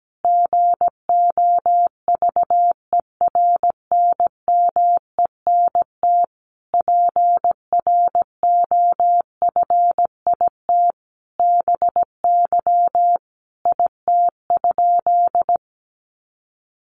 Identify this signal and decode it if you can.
Morse code, heard as a keyed tone, about 17 wpm, 700 Hz. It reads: GOVERNMENT PROFIT BY IT?
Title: Sentences from Top 800 Words 17wpm